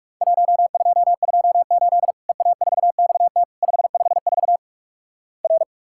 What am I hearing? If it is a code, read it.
Morse code, heard as a keyed tone, about 46 wpm, 700 Hz. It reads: 1228 EA4XT 554 R
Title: Worked All Europe DX Contest - QTC Exchange - 45wpm